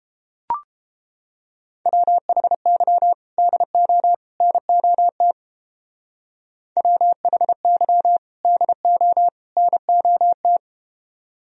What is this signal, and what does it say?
Morse code, heard as a keyed tone, about 30 wpm, 700 Hz. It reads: WHY DO NOT WHY DO NOT